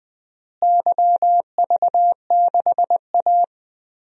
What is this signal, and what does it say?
Morse code, heard as a keyed tone, about 20 wpm, 700 Hz. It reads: YV6A
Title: DXCC-100 Call Signs - 4 Characters - Repeated 2x - 20wpm